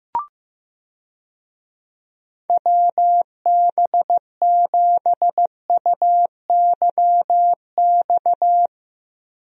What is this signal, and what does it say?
Morse code, heard as a keyed tone, about 15 wpm, 700 Hz. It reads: WB7UYX
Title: USA Call Signs - 6 Characters - Repeated 3x - 15wpm